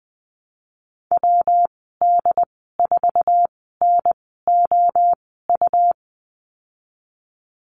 Morse code, heard as a keyed tone, about 20 wpm, spaced as for 14 wpm, 700 Hz. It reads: WD4NOU
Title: 122 Lesson 20g - Number 3 - Call Signs 20wpm